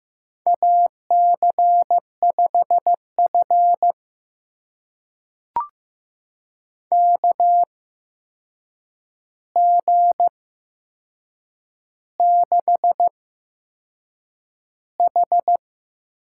Morse code, heard as a keyed tone, about 15 wpm, spaced as for 6 wpm, 700 Hz. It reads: AC5F KG6H
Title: USA Call Signs - 8x Character Spacing - 15wpm